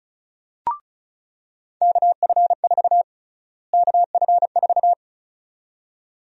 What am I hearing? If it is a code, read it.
Morse code, heard as a keyed tone, about 34 wpm, 700 Hz. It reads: KF4 KF4
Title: USA Call Sign Prefixes - Repeated 2x - 35wpm